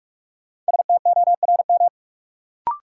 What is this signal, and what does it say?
Morse code, heard as a keyed tone, about 47 wpm, 700 Hz. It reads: STORM
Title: Sentences from Top 1500 Words - New Word Drill Alt - 45wpm